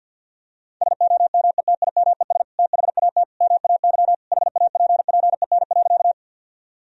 Morse code, heard as a keyed tone, about 50 wpm, 700 Hz. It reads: SOMETIMES THAT MAY HAPPEN.
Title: Sentences from Top 1000 Words - Review Part 3 - 50wpm